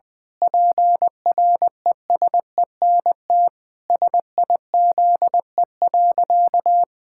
Morse code, heard as a keyed tone, about 20 wpm, 700 Hz. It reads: PRESENT SIZE.
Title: Sentences from Top 1000 Words - Review Part 6 - 20wpm